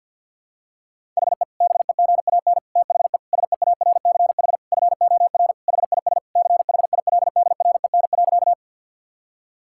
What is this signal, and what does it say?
Morse code, heard as a keyed tone, about 46 wpm, 690 Hz. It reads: HE BEGAN THE SEARCH FOR HIS CHILDREN.
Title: Sentences from Top 1000 Words - Review Part 5 - 50wpm